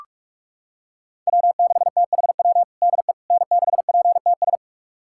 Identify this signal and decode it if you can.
Morse code, heard as a keyed tone, about 46 wpm, 700 Hz. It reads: W6THW DE N6PTS
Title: USA Callsign DE Callsign - 5 Characters - 45wpm